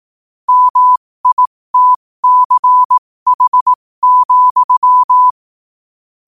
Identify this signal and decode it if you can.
Morse code, heard as a keyed tone, about 18 wpm, spaced as for 15 wpm, 1000 Hz. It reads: MITCH,